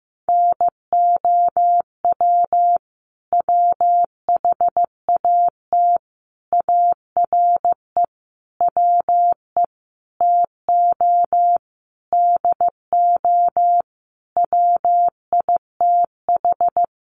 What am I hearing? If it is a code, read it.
Morse code, heard as a keyed tone, about 15 wpm, 700 Hz. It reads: NOW WHAT ARE WE TO DO WITH
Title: Sentences from Top 100 Words 15wpm